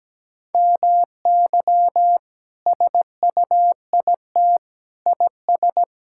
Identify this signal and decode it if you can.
Morse code, heard as a keyed tone, about 17 wpm, 700 Hz. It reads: MY SUIT IS
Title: Sentences from Top 1600 Words 17wpm